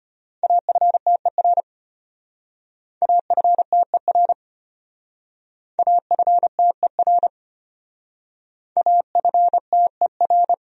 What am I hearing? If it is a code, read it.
Morse code, heard as a keyed tone, about 30 wpm, 700 Hz. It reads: AFTER AFTER AFTER AFTER
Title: Speed-Racing Top 500 Words 25wpm